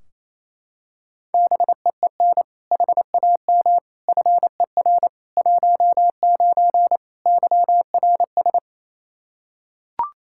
Morse code, heard as a keyed tone, about 28 wpm, 700 Hz. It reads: BEEN HAM FER 19 YRS
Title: Practice QSO - Exchange 3 - More Info - 28wpm